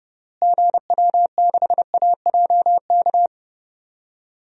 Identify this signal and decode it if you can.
Morse code, heard as a keyed tone, about 32 wpm, 700 Hz. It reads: GW6AJK
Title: DXCC-100 Call Signs - 6 Characters - Repeated 2x - 30wpm